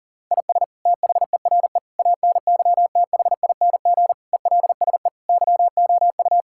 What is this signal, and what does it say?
Morse code, heard as a keyed tone, about 40 wpm, 700 Hz. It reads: IS THERE ANYTHING ELSE YOU
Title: Sentences from Top 1600 Words 40wpm